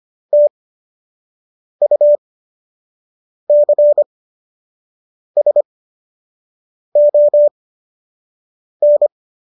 Morse code, heard as a keyed tone, about 25 wpm, spaced as for 6 wpm, 590 Hz. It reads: TUCSON